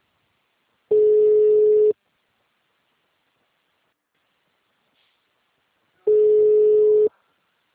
Ringback tone (Europe, 425 Hz).